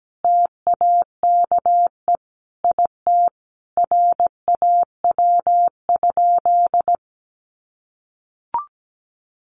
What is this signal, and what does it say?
Morse code, heard as a keyed tone, about 17 wpm, 700 Hz. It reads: TAKE IT RAW?
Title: Sentences from Top 1500 Words 17wpm